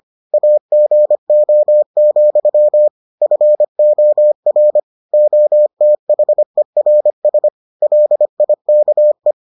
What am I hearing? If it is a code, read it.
Morse code, heard as a keyed tone, about 25 wpm, 600 Hz. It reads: AGO, FOR OTHERS LIKE